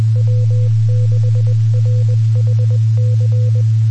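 Morse code, heard as a keyed tone, about 20 wpm, 500 Hz, with noise. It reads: W6RHC
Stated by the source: Morse code recorded on the 2 m band at 146.850 MHz.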